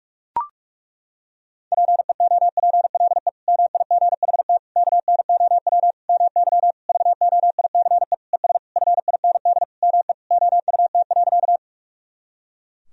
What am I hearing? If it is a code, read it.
Morse code, heard as a keyed tone, about 41 wpm, 700 Hz. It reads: PEOPLE MIGHT KNOW MY VOICE ES FIND ME OUT.
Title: Sentences from Top 500 Words and Abbreviations 45wpm